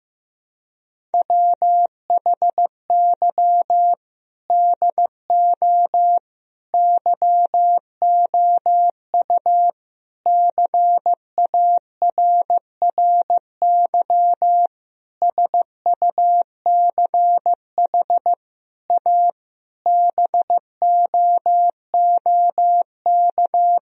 Morse code, heard as a keyed tone, about 15 wpm, 700 Hz. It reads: WHY DO YOU CARRY SUCH A BOOK
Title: Sentences from Top 1000 Words - Review Part 1 - 15wpm